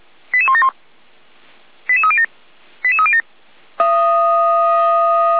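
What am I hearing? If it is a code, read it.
Five-tone selective call (ZVEI), heard as quick runs of five tones, tones of about 70 ms. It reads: 88171 88307 88307
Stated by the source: ZVEI five-tone selective call received on an SDR.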